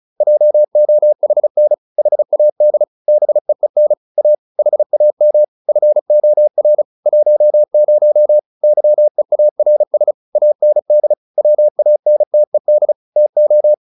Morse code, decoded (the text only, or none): JOHN HAD BEEN A HAM FOR 10 YEARS AND WANTED TO